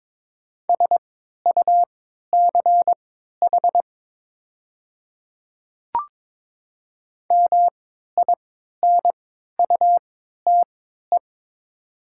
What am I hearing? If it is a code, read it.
Morse code, heard as a keyed tone, about 22 wpm, spaced as for 12 wpm, 700 Hz. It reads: SUCH MINUTE
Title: 3x Character Spacing - Top 500 Words - 22wpm